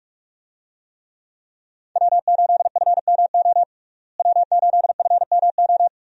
Morse code, heard as a keyed tone, about 43 wpm, 700 Hz. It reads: W8FMO W8FMO
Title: USA Call Signs - 5 Characters - Repeated 2x - 45wpm